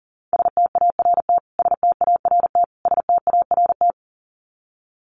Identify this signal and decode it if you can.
Morse code, heard as a keyed tone, about 40 wpm, 700 Hz. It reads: START START START
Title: Top 500 Words Repeated 40wpm